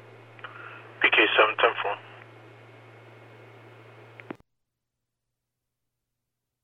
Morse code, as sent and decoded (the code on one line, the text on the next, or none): none